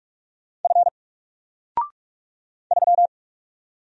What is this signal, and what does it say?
Morse code, heard as a keyed tone, about 44 wpm, 700 Hz. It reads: F 3
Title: Single Letter-Number Alt 45wpm